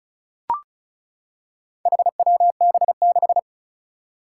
Morse code, heard as a keyed tone, about 35 wpm, 700 Hz. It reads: HWB6
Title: Four Challenging Letter-Number Combinations 35wpm